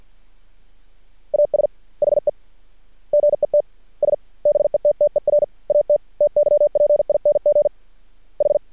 Morse code, heard as a keyed tone, about 48 wpm, spaced as for 41 wpm, 590 Hz. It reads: AS HE GETS BETTER AT TYPING H
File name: archived / CRSnet